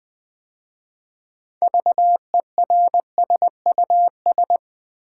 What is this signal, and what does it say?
Morse code, heard as a keyed tone, about 20 wpm, 690 Hz. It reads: VERSUS
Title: 6 Letter Words - First 3 Letters Given - 20wpm